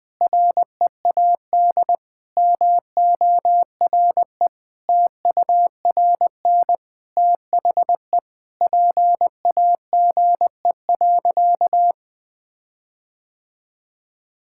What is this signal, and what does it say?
Morse code, heard as a keyed tone, about 20 wpm, 700 Hz. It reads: READ MORE TURN THE PAGE.
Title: Sentences with 6-Word Phrases - 20wpm